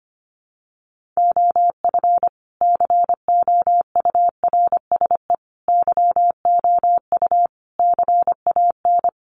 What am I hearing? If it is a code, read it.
Morse code, heard as a keyed tone, about 25 wpm, 700 Hz. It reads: OF COURSE YOU CAN
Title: Sentences from Top 1300 Words 25wpm